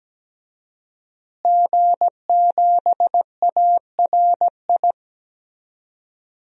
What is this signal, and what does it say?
Morse code, heard as a keyed tone, about 17 wpm, 700 Hz. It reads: G7ARI
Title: DXCC-100 Call Signs - 5 Characters - Repeated 3x - 17wpm